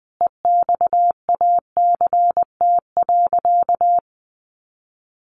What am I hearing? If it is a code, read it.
Morse code, heard as a keyed tone, about 20 wpm, 700 Hz. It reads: EXACT.
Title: Sentences from Top 1300 Words 20wpm